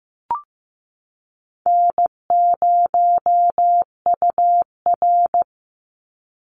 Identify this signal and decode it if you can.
Morse code, heard as a keyed tone, about 15 wpm, 700 Hz. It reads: N0UR
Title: Easy Call Signs - 15wpm